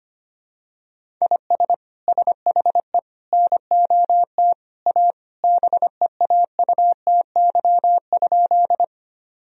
Morse code, heard as a keyed tone, about 25 wpm, 700 Hz. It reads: IS SHE NOT A BEAUTY?
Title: Sentences from Top 1000 Words - Review Part 2 - 25wpm